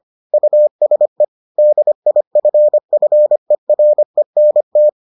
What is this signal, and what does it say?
Morse code, heard as a keyed tone, about 25 wpm, 600 Hz. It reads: USE DIFFERENT